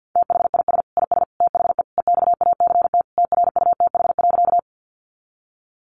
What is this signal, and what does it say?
Morse code, heard as a keyed tone, about 50 wpm, 700 Hz. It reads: THIS IS THE EXACT TRUTH.
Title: Sentences from Top 1300 Words 50wpm